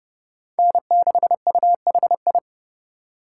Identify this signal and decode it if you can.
Morse code, heard as a keyed tone, about 30 wpm, 700 Hz. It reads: N6UHI
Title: USA Call Signs - 5 Characters - 2x Character Spacing - 30wpm